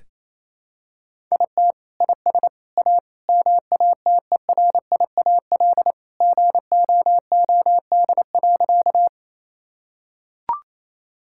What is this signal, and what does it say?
Morse code, heard as a keyed tone, about 28 wpm, 700 Hz. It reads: IT IS A MATERIAL GOOD.
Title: Sentences from Top 1000 Words - Review Part 5 - 28wpm